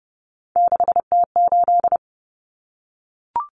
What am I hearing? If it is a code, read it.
Morse code, heard as a keyed tone, about 30 wpm, 700 Hz. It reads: 6T8
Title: Common Cut Numbers - 3-Digit 30wpm